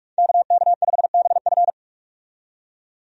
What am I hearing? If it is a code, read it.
Morse code, heard as a keyed tone, about 46 wpm, 700 Hz. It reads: KK5BF
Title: USA Call Signs - 5 Characters - 45wpm